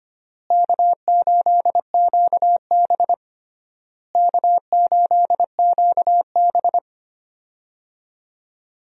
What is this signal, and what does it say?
Morse code, heard as a keyed tone, about 25 wpm, 700 Hz. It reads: K8QB K8QB
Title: USA Call Signs - 4 Characters - Repeated 3x - 25wpm